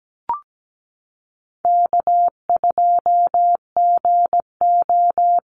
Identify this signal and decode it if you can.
Morse code, heard as a keyed tone, about 17 wpm, 700 Hz. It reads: K2GO
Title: Easy Call Signs Repeated 3x - 17wpm